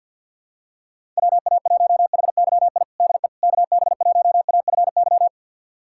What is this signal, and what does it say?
Morse code, heard as a keyed tone, about 53 wpm, 700 Hz. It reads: WA1HYI DE KB1AFY
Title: USA Callsign DE Callsign - 6 Characters - 50wpm